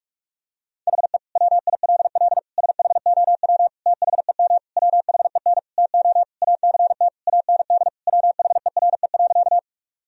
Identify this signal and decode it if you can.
Morse code, heard as a keyed tone, about 46 wpm, 700 Hz. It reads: HE WILL SHOW THEM WHEN TO ACT AND WHERE.
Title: Sentences from Top 1000 Words - Review Part 1 - 45wpm